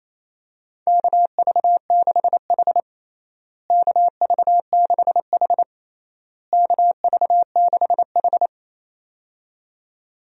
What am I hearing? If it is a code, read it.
Morse code, heard as a keyed tone, about 28 wpm, 700 Hz. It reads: KV6H KV6H KV6H
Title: USA Call Signs - 4 Characters - Repeated 3x - 28wpm